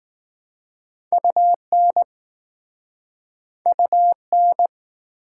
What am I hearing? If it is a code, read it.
Morse code, heard as a keyed tone, about 20 wpm, 700 Hz. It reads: UN UN